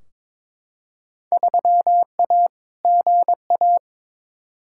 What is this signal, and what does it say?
Morse code, heard as a keyed tone, about 22 wpm, 700 Hz. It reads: 3A GA
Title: Field Day Exchanges - 22wpm